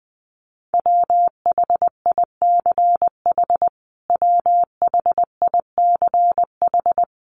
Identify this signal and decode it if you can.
Morse code, heard as a keyed tone, about 20 wpm, 700 Hz. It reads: WHICH WHICH
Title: Top 500 Words Repeated 20wpm